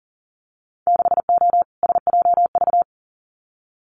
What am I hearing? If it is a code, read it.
Morse code, heard as a keyed tone, about 42 wpm, 700 Hz. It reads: 6O SJV
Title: Winter Field Day Exchanges Encore 40wpm